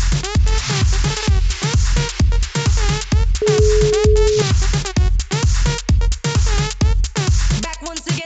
Ringback tone (Europe, 425 Hz), with noise.